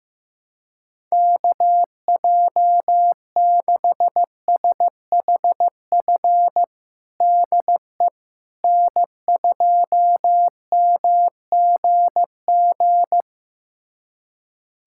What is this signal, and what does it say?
Morse code, heard as a keyed tone, about 15 wpm, 700 Hz. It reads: KJ6SHF DE N2MGG
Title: USA Callsign DE Callsign 15wpm